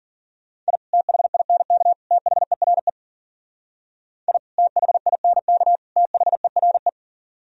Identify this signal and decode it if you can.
Morse code, heard as a keyed tone, about 51 wpm, 700 Hz. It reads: I THINK THERE I THINK THERE